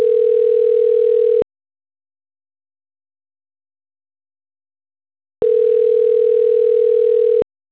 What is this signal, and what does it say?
Ringback tone (North America), 440 + 480 Hz.